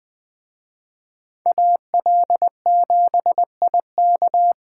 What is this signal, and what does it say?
Morse code, heard as a keyed tone, about 19 wpm, 700 Hz. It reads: AL7IK